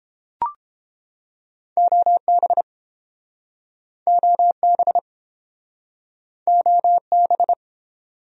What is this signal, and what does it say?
Morse code, heard as a keyed tone, about 31 wpm, 700 Hz. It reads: OB OB OB